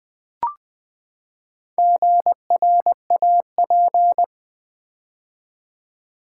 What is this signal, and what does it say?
Morse code, heard as a keyed tone, about 20 wpm, 700 Hz. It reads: GRAP
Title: Build A Word - 4 Letters At A Time - 8-Letter Words - 20wpm